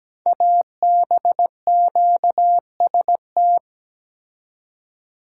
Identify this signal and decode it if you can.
Morse code, heard as a keyed tone, about 17 wpm, 700 Hz. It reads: ABQST
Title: Code Groups - 5 Letters - 17wpm